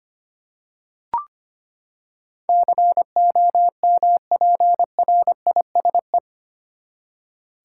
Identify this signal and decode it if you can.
Morse code, heard as a keyed tone, about 25 wpm, 700 Hz. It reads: COMPRISE
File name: Eight Letter Words - 25wpm